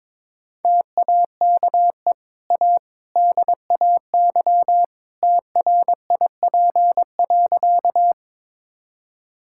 Morse code, heard as a keyed tone, about 21 wpm, 690 Hz. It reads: TAKE A DAY TRIP.
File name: practice / 22wpm / Easy Sentences with Hints - 22wpm